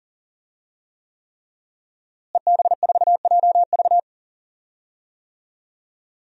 Morse code, heard as a keyed tone, about 40 wpm, 700 Hz. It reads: EB4JV
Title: DXCC-100 Call Signs - 5 Characters - Repeated 3x - 40wpm